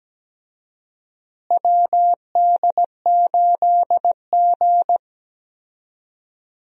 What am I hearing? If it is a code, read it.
Morse code, heard as a keyed tone, about 17 wpm, 700 Hz. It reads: WD8G
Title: USA Call Signs - 4 Characters - Repeated 2x - 17wpm